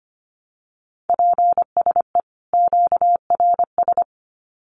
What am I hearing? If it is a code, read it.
Morse code, heard as a keyed tone, about 25 wpm, 700 Hz. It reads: PSE QRS
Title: Common QSO Phrases 25wpm